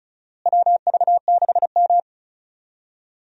Morse code, heard as a keyed tone, about 34 wpm, 700 Hz. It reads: WV6M
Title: USA Call Signs - 4 Characters - Repeated 3x - 35wpm